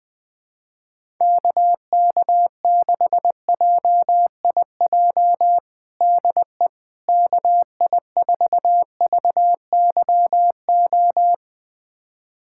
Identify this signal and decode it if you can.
Morse code, heard as a keyed tone, about 20 wpm, 700 Hz. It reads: KK6JIJ DE KI4VYO
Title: USA Callsign DE Callsign - 6 Characters - 20wpm